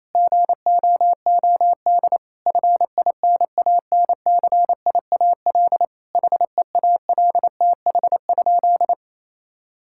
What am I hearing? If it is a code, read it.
Morse code, heard as a keyed tone, about 28 wpm, 700 Hz. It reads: GOOD FINANCIAL HEALTH?
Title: Sentences from Top 1000 Words - Review Part 5 - 28wpm